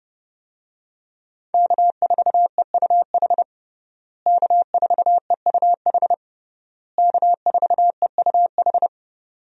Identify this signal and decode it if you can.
Morse code, heard as a keyed tone, about 30 wpm, 700 Hz. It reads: K4EUH K4EUH K4EUH
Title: USA Call Signs - 5 Characters - Repeated 3x - 30wpm